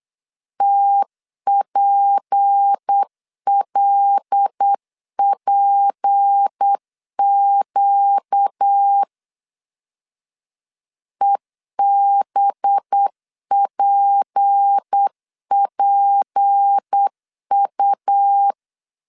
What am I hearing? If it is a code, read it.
Morse code, heard as a keyed tone, about 8 wpm, 800 Hz. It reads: TPLPQ EBPPU
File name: LEZIONI / 6 PKX0 / 6 P 8wpm 3025